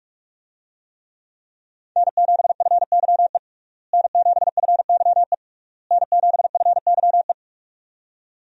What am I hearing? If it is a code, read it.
Morse code, heard as a keyed tone, about 46 wpm, 700 Hz. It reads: N7FYE N7FYE N7FYE
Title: USA Call Signs - 5 Characters - Repeated 3x - 45wpm